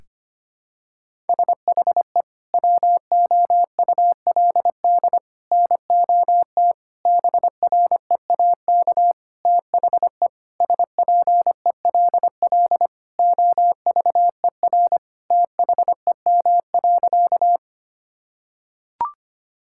Morse code, heard as a keyed tone, about 25 wpm, 700 Hz. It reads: SHE WOULD NOT BREAK THE SPELL OVER THEM.
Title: Sentences from Top 1000 Words - Review Part 6 - 25wpm